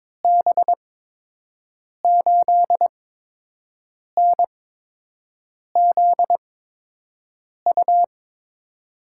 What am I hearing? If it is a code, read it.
Morse code, heard as a keyed tone, about 22 wpm, spaced as for 6 wpm, 700 Hz. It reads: B8NZU
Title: USA Call Signs - 6 Characters - 8x Character Spacing - 22wpm